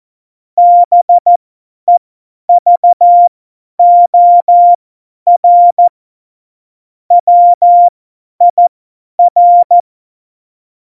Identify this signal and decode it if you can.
Morse code, heard as a keyed tone, about 14 wpm, spaced as for 10 wpm, 700 Hz. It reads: BEVOR WIR
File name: Morsekurs / Witze / Witz 14. August 2018, 50 F70